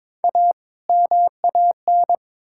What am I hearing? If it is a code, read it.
Morse code, heard as a keyed tone, about 22 wpm, 700 Hz. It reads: A MAN
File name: Build A Phrase - 2-Word Phrases - 22wpm